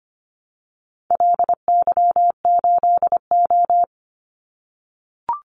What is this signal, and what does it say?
Morse code, heard as a keyed tone, about 25 wpm, 700 Hz. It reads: LY8O
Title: Call Signs - 25wpm